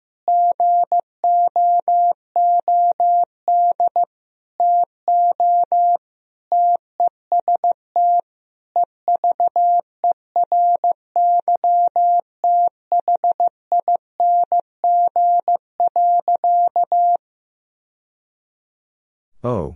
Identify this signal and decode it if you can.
Morse code, heard as a keyed tone, about 15 wpm, 700 Hz. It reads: GOOD TO TEST EVERYTHING.
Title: Sentences from Top 1000 Words - Review Part 3 - 15wpm